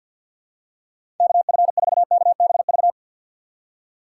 Morse code, heard as a keyed tone, about 53 wpm, 700 Hz. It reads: KF4KBV
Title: USA Call Signs - 6 Characters - 4x Character Spacing - 50wpm